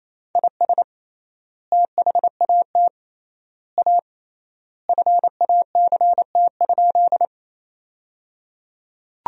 Morse code, heard as a keyed tone, about 28 wpm, 700 Hz. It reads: IS THAT A FACT?
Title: 3x Word-Spacing - Sentences from Top 500 Words - 28wpm